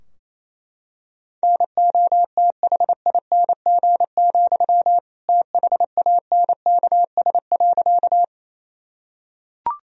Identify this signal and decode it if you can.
Morse code, heard as a keyed tone, about 28 wpm, 700 Hz. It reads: NOTHING, THANKS.
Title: Sentences from Top 1000 Words - Review Part 3 - 28wpm